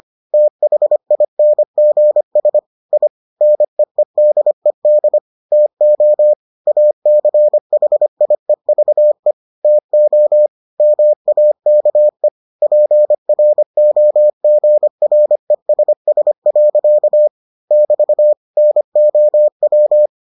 Morse code, heard as a keyed tone, about 25 wpm, 600 Hz. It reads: THINGS I NEEDED TO ACHIEVE TO MAKE PROGRESS. <BT> NOW